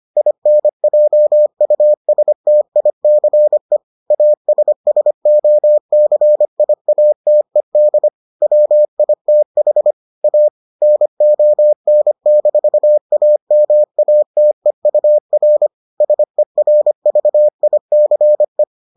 Morse code, heard as a keyed tone, about 25 wpm, 600 Hz. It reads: INJUSTICE ASSOCIATED WITH A NON-AMATEUR SERVICE